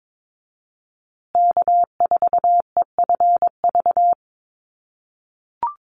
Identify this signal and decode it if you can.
Morse code, heard as a keyed tone, about 22 wpm, 700 Hz. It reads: K4EFV
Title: USA Call Signs Repeated 3x 22wpm